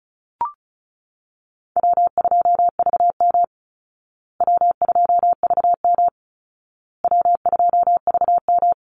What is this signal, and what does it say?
Morse code, heard as a keyed tone, about 34 wpm, 700 Hz. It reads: W2VM W2VM W2VM
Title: Call Signs Repeated 2x - 35wpm